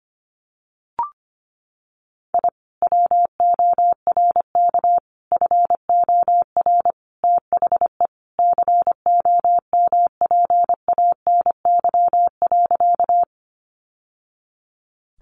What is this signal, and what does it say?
Morse code, heard as a keyed tone, about 25 wpm, 700 Hz. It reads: I WORK FOR THE COMPANY.
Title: Sentences from Top 600 Words 25wpm